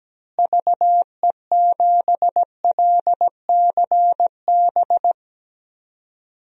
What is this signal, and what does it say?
Morse code, heard as a keyed tone, about 17 wpm, 700 Hz. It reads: VE7LCB
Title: DXCC-100 Call Signs - 6 Characters - Repeated 3x - 17wpm